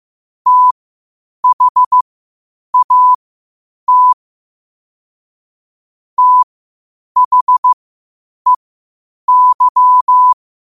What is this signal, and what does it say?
Morse code, heard as a keyed tone, about 15 wpm, spaced as for 8 wpm, 1000 Hz. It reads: THAT THEY
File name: qotdfiles / 8wpm